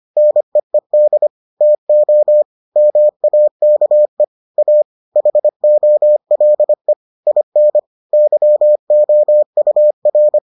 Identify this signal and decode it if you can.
Morse code, heard as a keyed tone, about 25 wpm, 600 Hz. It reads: NEED TO MAKE A HOLE IN YOUR